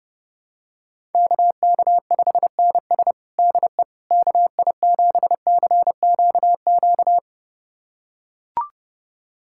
Morse code, heard as a keyed tone, about 32 wpm, 700 Hz. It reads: KK5NS DE KI7CQQ
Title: USA Callsign DE Callsign 30wpm